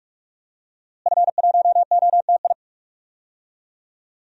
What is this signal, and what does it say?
Morse code, heard as a keyed tone, about 46 wpm, 700 Hz. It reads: F1OTI
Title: DXCC-100 Call Signs - 5 Characters - Repeated 2x - 45wpm